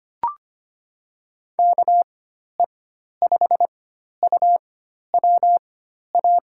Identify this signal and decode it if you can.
Morse code, heard as a keyed tone, about 25 wpm, spaced as for 12 wpm, 700 Hz. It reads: KE5UWA
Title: USA Call Signs - 6 Characters - 4x Character Spacing - 25wpm